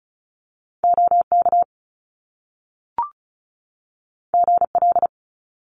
Morse code, heard as a keyed tone, about 36 wpm, 700 Hz. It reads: OK GL
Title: Top Two Letter Combinations in Words 35wpm